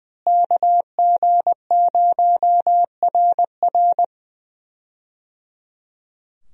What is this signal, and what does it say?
Morse code, heard as a keyed tone, about 20 wpm, 700 Hz. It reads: KG0RR
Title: USA Call Signs - 5 Characters - Repeated 3x - 20wpm